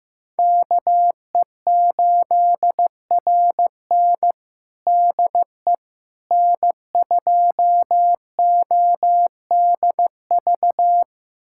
Morse code, heard as a keyed tone, about 15 wpm, 700 Hz. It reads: KE8RN DE N2ODV
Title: USA Callsign DE Callsign - 5 Characters - 15wpm